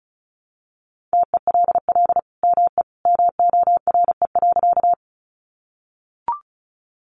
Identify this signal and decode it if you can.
Morse code, heard as a keyed tone, about 34 wpm, 700 Hz. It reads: TELL ME MORE.
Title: Sentences from Top 300 Words 35wpm